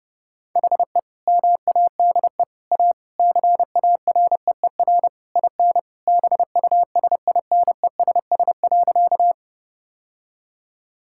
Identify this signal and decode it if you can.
Morse code, heard as a keyed tone, about 30 wpm, 700 Hz. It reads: HE MADE A CAREER IN BUSINESS.
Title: Sentences from Top 1000 Words - Review Part 6 - 30wpm